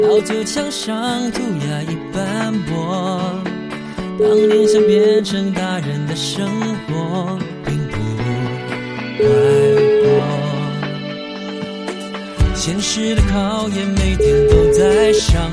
Ringback tone (Europe, 425 Hz), with noise.